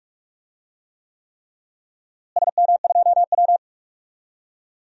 Morse code, heard as a keyed tone, about 43 wpm, 700 Hz. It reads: SM2W
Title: DXCC-100 Call Signs - 4 Characters - Repeated 3x - 45wpm